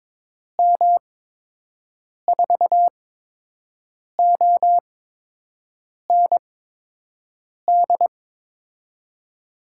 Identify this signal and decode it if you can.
Morse code, heard as a keyed tone, about 22 wpm, spaced as for 6 wpm, 700 Hz. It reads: M4OND
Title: USA Call Signs - 6 Characters - 8x Character Spacing - 22wpm